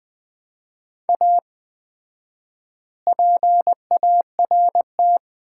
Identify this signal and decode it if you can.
Morse code, heard as a keyed tone, about 20 wpm, 700 Hz. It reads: A PART
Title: 4x Word-Spacing - Sets of 3 Words - 20wpm